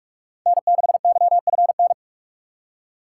Morse code, heard as a keyed tone, about 44 wpm, 700 Hz. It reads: N6YFN